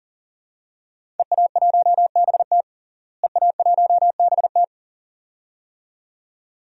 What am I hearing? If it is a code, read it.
Morse code, heard as a keyed tone, about 40 wpm, 700 Hz. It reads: EA1BT EA1BT
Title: DXCC-100 Call Signs - 5 Characters - Repeated 2x - 40wpm